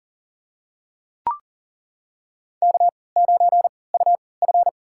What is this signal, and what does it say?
Morse code, heard as a keyed tone, about 40 wpm, spaced as for 23 wpm, 700 Hz. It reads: K9UF
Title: USA Call Signs - 3x Character Spacing - 40wpm